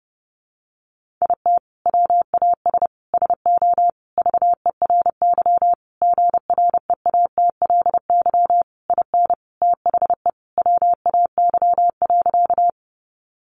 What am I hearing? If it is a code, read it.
Morse code, heard as a keyed tone, about 30 wpm, 700 Hz. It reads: IT WAS SO VERY GREATLY IN THE WAY.
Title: Sentences from Top 1400 Words 30wpm